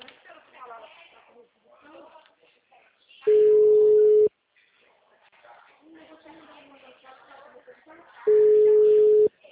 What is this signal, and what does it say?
Ringback tone (Europe, 425 Hz), with speech.
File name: out- -2002- - - .27347